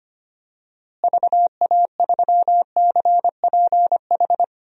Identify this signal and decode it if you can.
Morse code, heard as a keyed tone, about 25 wpm, 700 Hz. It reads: VA3CPH
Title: DXCC-100 Call Signs - 6 Characters - Repeated 3x - 25wpm